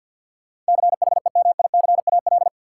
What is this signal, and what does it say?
Morse code, heard as a keyed tone, about 47 wpm, 700 Hz. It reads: CHEMICAL
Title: Sentences from Top 1200 Words - New Word Drill - 50wpm